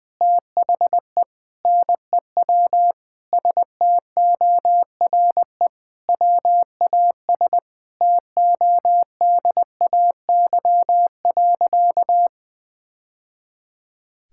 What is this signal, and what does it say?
Morse code, heard as a keyed tone, about 20 wpm, 700 Hz. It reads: THE NEW STORE WAS TODAY.
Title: Sentences with 3-Word Phrases - 20wpm